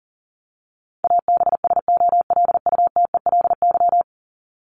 Morse code, heard as a keyed tone, about 40 wpm, 700 Hz. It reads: ABSOLUTELY
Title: Top 1501-1750 Words 40wpm